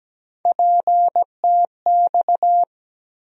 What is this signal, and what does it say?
Morse code, heard as a keyed tone, about 17 wpm, 700 Hz. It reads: PTX
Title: USA Call Sign Postfixes - 3 Letters 17wpm